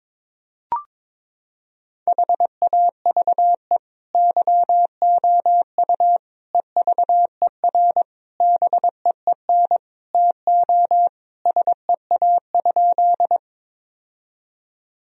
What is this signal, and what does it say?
Morse code, heard as a keyed tone, about 22 wpm, 700 Hz. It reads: HAVE YOU EVER BEEN TO SEA?
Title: Sentences from Top 1000 Words - Review Part 1 - 22wpm